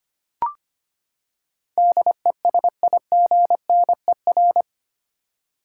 Morse code, heard as a keyed tone, about 25 wpm, 700 Hz. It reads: DESIGNER
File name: Eight Letter Words - 25wpm